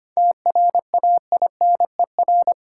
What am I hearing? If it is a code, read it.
Morse code, heard as a keyed tone, about 25 wpm, 700 Hz. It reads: TRAINER
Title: Seven Letter Words - 25wpm